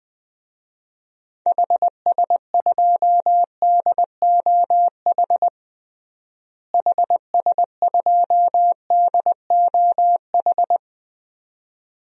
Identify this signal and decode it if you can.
Morse code, heard as a keyed tone, about 20 wpm, 700 Hz. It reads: HS2DOH HS2DOH
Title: DXCC-100 Call Signs - 6 Characters - Repeated 2x - 20wpm